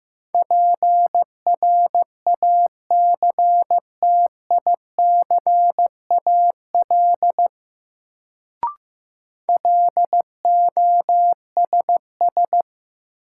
Morse code, heard as a keyed tone, about 15 wpm, 700 Hz. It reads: PRACTICAL LOSS
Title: Sentences from Top 1400 Words - New Word Drill - 15wpm